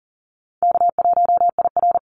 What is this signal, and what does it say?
Morse code, heard as a keyed tone, about 40 wpm, 700 Hz. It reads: K1IR
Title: Call Signs Repeated 3x - 40wpm